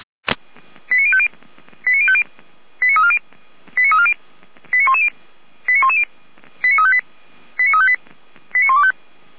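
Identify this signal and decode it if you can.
Five-tone selective call (ZVEI), heard as quick runs of five tones, tones of about 70 ms. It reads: 89050 89050 89340 89340 89110 89110 89358 89358 89125
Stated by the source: ZVEI five-tone call received by SDR, archived by fire brigade Feuerwehr Orlamünde.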